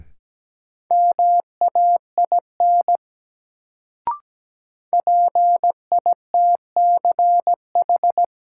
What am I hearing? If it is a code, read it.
Morse code, heard as a keyed tone, about 17 wpm, 700 Hz. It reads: MAIN PITCH
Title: Sentences from Top 1300 Words - New Word Drill Alt - 17wpm